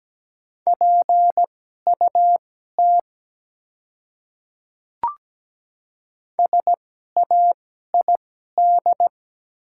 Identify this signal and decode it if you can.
Morse code, heard as a keyed tone, about 17 wpm, spaced as for 12 wpm, 700 Hz. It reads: PUT SAID
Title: 2x Character Spacing - Top 200 Words - 17wpm